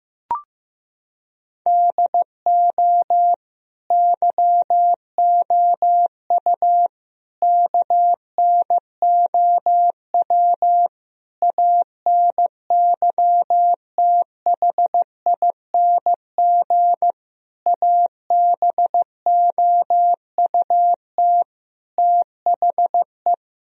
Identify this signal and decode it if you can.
Morse code, heard as a keyed tone, about 15 wpm, 700 Hz. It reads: DO YOU KNOW ANYTHING ABOUT THE
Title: Sentences from Top 1000 Words - Review Part 1 - 15wpm